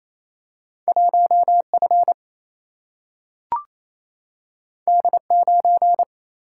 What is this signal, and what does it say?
Morse code, heard as a keyed tone, about 28 wpm, 700 Hz. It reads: 1F D9
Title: Single-Letter Single-Number 28wpm